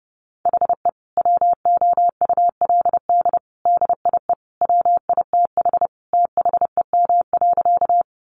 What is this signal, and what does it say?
Morse code, heard as a keyed tone, about 28 wpm, 700 Hz. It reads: HE WOULD DIE WITH THEM.
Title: Sentences from Top 1300 Words 30wpm